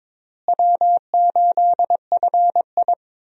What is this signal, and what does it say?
Morse code, heard as a keyed tone, about 22 wpm, 700 Hz. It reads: W8FI
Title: USA Call Signs - 4 Characters - Repeated 3x - 22wpm